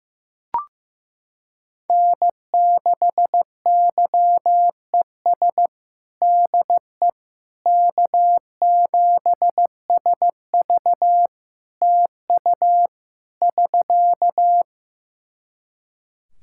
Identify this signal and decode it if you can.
Morse code, heard as a keyed tone, about 15 wpm, 700 Hz. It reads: N6YES DE K7SV TU <SK>
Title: K1USN SST QSO Practice 15wpm